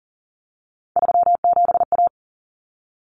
Morse code, heard as a keyed tone, about 40 wpm, 700 Hz. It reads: 37A
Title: Common Cut Numbers - 3-Digit 40wpm